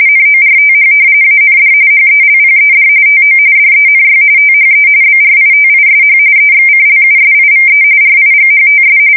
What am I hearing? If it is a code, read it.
RTTY, about 45.45 baud. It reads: NORTH AND HELPED TO DENY THE BRITISH EXCLUSIVE RIGHTS
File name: fsk170 / usnavy2